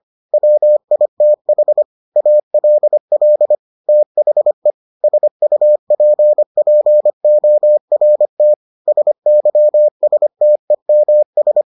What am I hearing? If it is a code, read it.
Morse code, heard as a keyed tone, about 25 wpm, 600 Hz. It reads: WITH ALL THE SUPPORT SYSTEMS